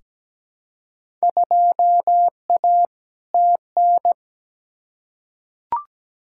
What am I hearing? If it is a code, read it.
Morse code, heard as a keyed tone, about 17 wpm, 700 Hz. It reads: 2A TN
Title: Field Day Exchanges - 17wpm